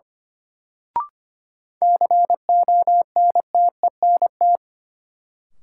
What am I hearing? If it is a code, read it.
Morse code, heard as a keyed tone, about 25 wpm, 700 Hz. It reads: CONTENT
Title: Sentences from Top 900 Words - New Word Drill - 25wpm